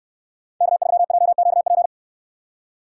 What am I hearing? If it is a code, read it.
Morse code, heard as a keyed tone, about 68 wpm, 700 Hz. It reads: K3JQP